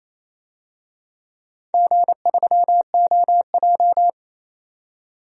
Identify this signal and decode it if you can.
Morse code, heard as a keyed tone, about 28 wpm, 700 Hz. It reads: G3OJ
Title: DXCC-100 Call Signs - 4 Characters - 28wpm